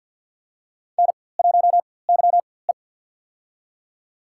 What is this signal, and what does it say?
Morse code, heard as a keyed tone, about 50 wpm, spaced as for 23 wpm, 700 Hz. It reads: N1YE
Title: USA Call Signs - 4x Character Spacing - 50wpm